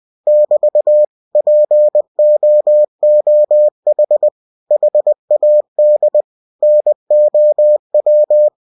Morse code, heard as a keyed tone, about 20 wpm, 600 Hz. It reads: <BT> POOH HAD NOW
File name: cw-books / winnie-the-pooh / winnie-20wpm-page-0057